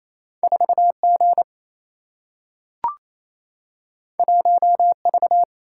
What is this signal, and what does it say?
Morse code, heard as a keyed tone, about 28 wpm, 700 Hz. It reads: 4G 1V
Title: Single-Letter Single-Number 28wpm